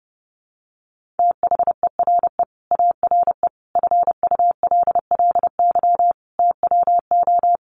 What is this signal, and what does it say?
Morse code, heard as a keyed tone, about 30 wpm, 700 Hz. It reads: THERE ARE FULLY TWO
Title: Sentences from Top 1300 Words 30wpm